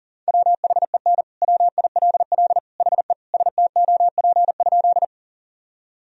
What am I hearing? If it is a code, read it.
Morse code, heard as a keyed tone, about 40 wpm, 700 Hz. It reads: WHEN WILL HE STOP?
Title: Sentences from Top 1000 Words - Review Part 1 - 40wpm